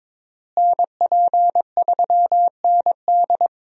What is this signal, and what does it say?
Morse code, heard as a keyed tone, about 22 wpm, 700 Hz. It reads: NP3ND
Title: USA Call Signs - 5 Characters - Repeated 3x - 22wpm